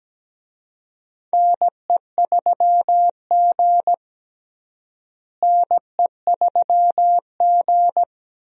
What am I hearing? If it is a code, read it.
Morse code, heard as a keyed tone, about 17 wpm, 700 Hz. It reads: NE3G NE3G
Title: USA Call Signs - 4 Characters - Repeated 3x - 17wpm